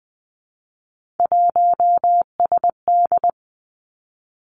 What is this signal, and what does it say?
Morse code, heard as a keyed tone, about 20 wpm, 700 Hz. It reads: 1SD
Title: Three Challenging Letter-Number Combinations 20wpm